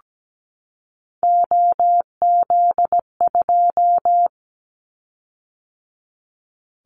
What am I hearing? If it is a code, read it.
Morse code, heard as a keyed tone, about 17 wpm, 700 Hz. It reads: OZ2
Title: Callsign Prefixes for Top 50 DXCC - 17wpm